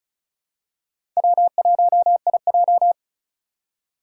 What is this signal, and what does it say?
Morse code, heard as a keyed tone, about 36 wpm, 700 Hz. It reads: W1IJ
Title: USA Call Signs - 2x Character Spacing - 35wpm